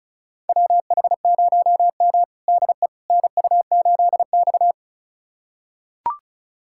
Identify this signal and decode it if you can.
Morse code, heard as a keyed tone, about 36 wpm, 700 Hz. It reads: WH0M DE NU8X
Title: USA Callsign DE Callsign - 4 Characters - 35wpm